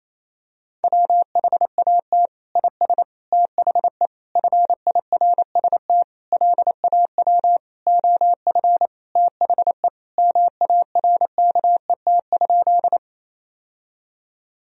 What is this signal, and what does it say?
Morse code, heard as a keyed tone, about 28 wpm, 700 Hz. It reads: WHAT IS THE FIRST LAW OF THE MARKET?
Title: Sentences from Top 1000 Words - Review Part 3 - 28wpm